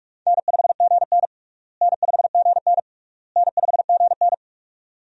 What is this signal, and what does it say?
Morse code, heard as a keyed tone, about 46 wpm, 700 Hz. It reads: N5GN N5GN N5GN
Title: USA Call Signs - 4 Characters - Repeated 3x - 45wpm